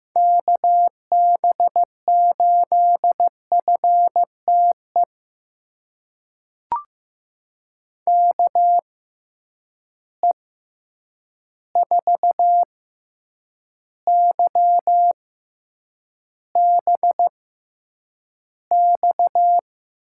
Morse code, heard as a keyed tone, about 15 wpm, spaced as for 8 wpm, 700 Hz. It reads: KB8FTE KE4YBX
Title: USA Call Signs - 6 Characters - 6x Character Spacing - 15wpm